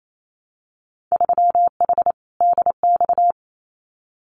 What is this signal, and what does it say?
Morse code, heard as a keyed tone, about 28 wpm, 700 Hz. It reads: 3H DX
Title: Winter Field Day Exchanges 28wpm